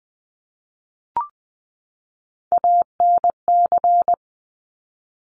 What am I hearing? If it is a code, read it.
Morse code, heard as a keyed tone, about 20 wpm, 700 Hz. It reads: ANC
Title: Top Three Letter Combinations in Words 20wpm